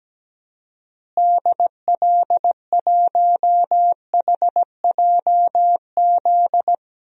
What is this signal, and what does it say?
Morse code, heard as a keyed tone, about 17 wpm, 700 Hz. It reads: DL1HJZ
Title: DXCC-100 Call Signs - 6 Characters - Repeated 3x - 17wpm